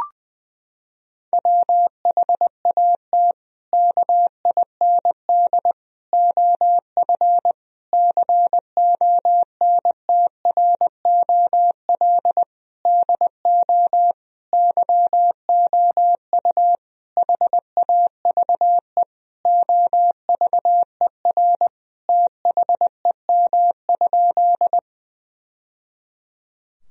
Morse code, heard as a keyed tone, about 20 wpm, 700 Hz. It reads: WHAT KIND OF CONTROL DO YOU HAVE OVER THEM?
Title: Sentences from Top 1000 Words - Review Part 3 - 20wpm